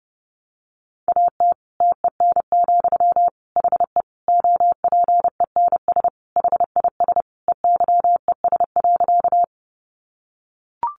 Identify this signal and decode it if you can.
Morse code, heard as a keyed tone, about 28 wpm, 700 Hz. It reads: AT TEN, HE OPENS HIS EYES.
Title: Easy Phrases 30wpm